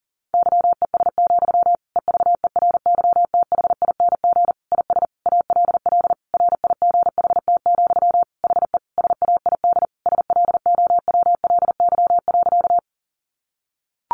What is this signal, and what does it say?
Morse code, heard as a keyed tone, about 37 wpm, 700 Hz. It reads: YES, EVERYTHING IS ALL RIGHT, HE SAID SLOWLY.
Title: Sentences from Top 1100 Words 40wpm